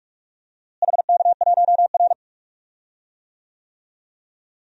Morse code, heard as a keyed tone, about 46 wpm, 700 Hz. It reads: HK1R
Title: DXCC-100 Call Signs - 4 Characters - Repeated 3x - 45wpm